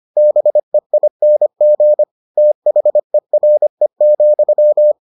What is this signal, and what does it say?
Morse code, heard as a keyed tone, about 25 wpm, 600 Hz. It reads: BEING THERE,